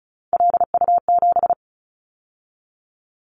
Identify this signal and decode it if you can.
Morse code, heard as a keyed tone, about 34 wpm, 700 Hz. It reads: LU7
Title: Callsign Prefixes for Top 50 DXCC - 35wpm